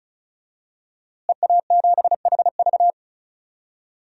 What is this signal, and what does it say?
Morse code, heard as a keyed tone, about 36 wpm, 700 Hz. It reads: EA7HV
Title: DXCC-100 Call Signs - 5 Characters - Repeated 2x - 35wpm